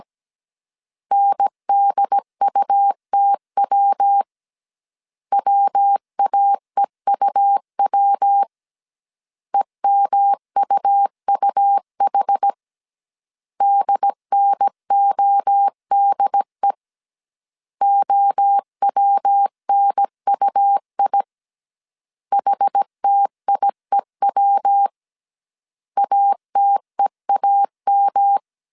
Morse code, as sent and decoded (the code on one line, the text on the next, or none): -. -.. ..- - .-- / .-- .- . ..- .-- / . -- ..- ..- .... / -.. -. --- -.. / --- .-- -. ..- .. / .... - .. .-- / .- - . .- --
NDUTW WAEUW EMUUH DNODE OWNUI HTIEW ATEAM